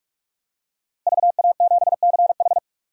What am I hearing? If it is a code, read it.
Morse code, heard as a keyed tone, about 46 wpm, 700 Hz. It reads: VA7CH
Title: DXCC-100 Call Signs - 5 Characters - Repeated 2x - 45wpm